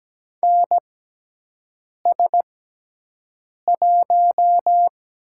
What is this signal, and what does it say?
Morse code, heard as a keyed tone, about 17 wpm, spaced as for 6 wpm, 700 Hz. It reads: NS1
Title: USA Call Signs - 5 Characters - 6x Character Spacing - 17wpm